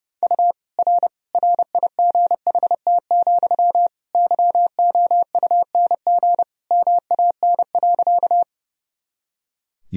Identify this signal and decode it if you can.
Morse code, heard as a keyed tone, about 30 wpm, 700 Hz. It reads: U R RIGHT, YOUNG MAN.
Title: Sentences from Top 400 Words and Abbreviations 30wpm